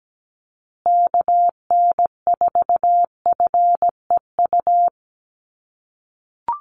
Morse code, heard as a keyed tone, about 17 wpm, 700 Hz. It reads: KN4FEU
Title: USA Call Signs Repeated 2x 17wpm